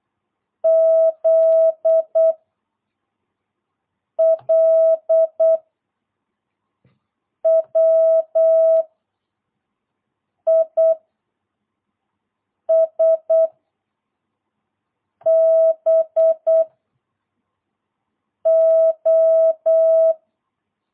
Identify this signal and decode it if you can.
Morse code, heard as a keyed tone, about 8 wpm, 660 Hz. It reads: Z L W I S B O
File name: jh8cbh / 2022cw